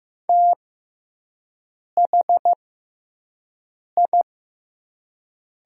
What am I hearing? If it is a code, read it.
Morse code, heard as a keyed tone, about 15 wpm, spaced as for 5 wpm, 700 Hz. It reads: THI
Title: 6x Character Spacing - Top 300 Words - 15wpm